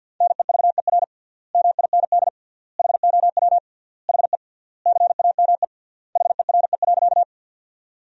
Morse code, heard as a keyed tone, about 50 wpm, 700 Hz. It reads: NEVER MIND HOW HE CAME HERE.
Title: 3x Word-Spacing - Sentences from Top 500 Words - 50wpm